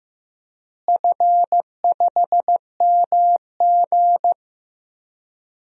Morse code, heard as a keyed tone, about 15 wpm, 700 Hz. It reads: F5MG
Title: DXCC-100 Call Signs - 4 Characters - Repeated 2x - 15wpm